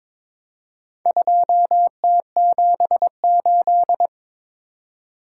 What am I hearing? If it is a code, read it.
Morse code, heard as a keyed tone, about 22 wpm, 700 Hz. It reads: 2T78
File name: practice / 22wpm / Four Letter-Number Combinations - 22wpm